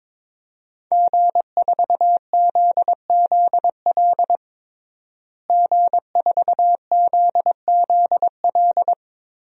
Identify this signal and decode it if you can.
Morse code, heard as a keyed tone, about 22 wpm, 700 Hz. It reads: G4ZZL G4ZZL
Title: DXCC-100 Call Signs - 5 Characters - Repeated 3x - 22wpm